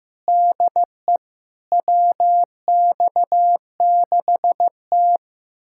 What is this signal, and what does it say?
Morse code, heard as a keyed tone, about 15 wpm, 700 Hz. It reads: DE WX6T
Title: USA Callsign DE Callsign - 4 Characters - 15wpm